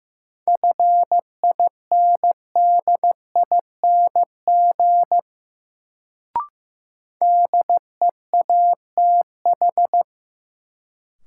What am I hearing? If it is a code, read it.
Morse code, heard as a keyed tone, about 15 wpm, 700 Hz. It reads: FINDING DEATH
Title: Sentences from Top 1100 Words - New Word Drill - 15wpm